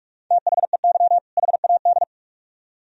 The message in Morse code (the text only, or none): THEY HAD